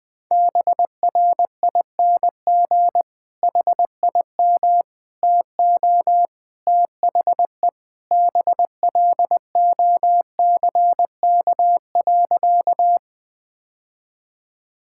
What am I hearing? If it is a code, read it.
Morse code, heard as a keyed tone, about 21 wpm, 700 Hz. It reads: BRING HIM TO THE BLOCK.
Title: Sentences from Top 1600 Words 20wpm